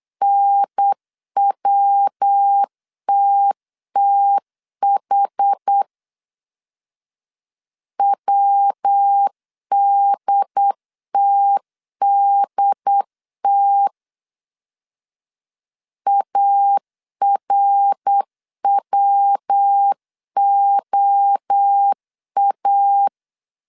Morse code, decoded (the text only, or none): NWTTH WDTDT ARWOA